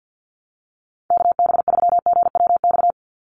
Morse code, heard as a keyed tone, about 50 wpm, 700 Hz. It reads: KB3GWX
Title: USA Call Signs 50wpm